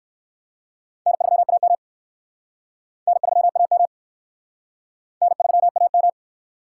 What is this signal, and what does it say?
Morse code, heard as a keyed tone, about 55 wpm, 700 Hz. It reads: N3AM N3AM N3AM
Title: Speed Racing - USA 1x2 Call Signs 45wpm